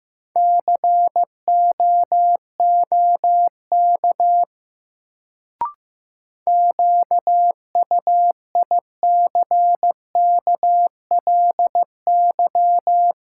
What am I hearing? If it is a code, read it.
Morse code, heard as a keyed tone, about 15 wpm, 700 Hz. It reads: COOK QUICKLY
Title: Sentences from Top 700 Words - New Word Drill - 15wpm